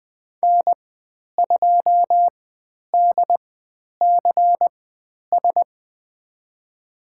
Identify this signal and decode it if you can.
Morse code, heard as a keyed tone, about 20 wpm, spaced as for 10 wpm, 700 Hz. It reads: N2DCS
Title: 109 Lesson 18g - Letter m - Call Signs 20wpm